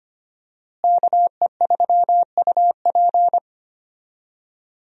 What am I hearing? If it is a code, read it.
Morse code, heard as a keyed tone, about 25 wpm, 700 Hz. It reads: KE3UP
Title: USA Call Signs - 5 Characters - Repeated 3x - 25wpm